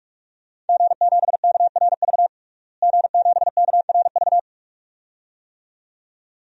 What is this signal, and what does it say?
Morse code, heard as a keyed tone, about 41 wpm, 700 Hz. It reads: G7KRV G7KRV
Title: DXCC-100 Call Signs - 5 Characters - Repeated 2x - 45wpm